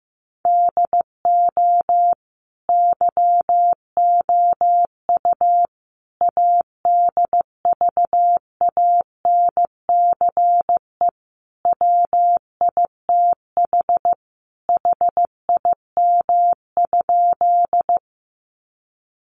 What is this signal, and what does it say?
Morse code, heard as a keyed tone, about 15 wpm, 700 Hz. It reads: DO YOU ADVANCE WITH HIM?
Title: Sentences from Top 1400 Words 15wpm